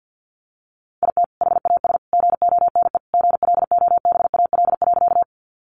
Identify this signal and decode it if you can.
Morse code, heard as a keyed tone, about 50 wpm, 700 Hz. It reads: IT HAS GONE GLOBAL.
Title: Sentences from Top 1200 Words 50wpm